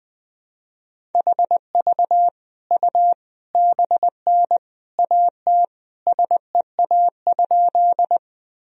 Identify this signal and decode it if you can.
Morse code, heard as a keyed tone, about 20 wpm, 700 Hz. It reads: HV U BN AT SEA?
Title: Sentences from Top 300 Words and Abbreviations 20wpm